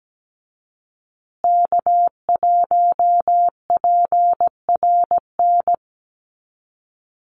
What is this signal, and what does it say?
Morse code, heard as a keyed tone, about 17 wpm, 700 Hz. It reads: K1PRN
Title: USA Call Signs Repeated 2x 17wpm